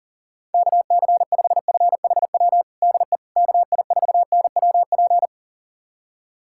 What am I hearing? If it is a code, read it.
Morse code, heard as a keyed tone, about 37 wpm, 700 Hz. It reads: KC5FHW DE KI4NWP
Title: USA Callsign DE Callsign 40wpm